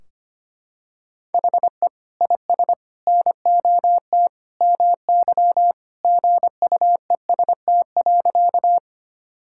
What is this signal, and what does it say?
Morse code, heard as a keyed tone, about 25 wpm, 700 Hz. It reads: HE IS NOT MY GUEST.
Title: Sentences from Top 1600 Words 25wpm